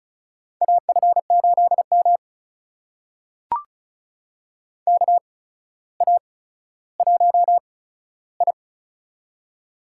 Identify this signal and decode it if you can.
Morse code, heard as a keyed tone, about 36 wpm, spaced as for 15 wpm, 700 Hz. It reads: AF8M KA1I
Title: USA Call Signs - 4 Characters - 8x Character Spacing - 35wpm